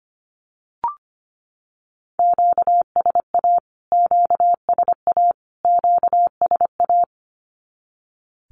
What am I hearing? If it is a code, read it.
Morse code, heard as a keyed tone, about 25 wpm, 700 Hz. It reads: QSA QSA QSA
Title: Q-Codes Repeated 3x 25wpm